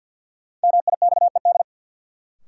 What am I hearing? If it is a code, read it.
Morse code, heard as a keyed tone, about 52 wpm, 700 Hz. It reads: MIXED
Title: Sentences from Top 1500 Words - New Word Drill Alt - 50wpm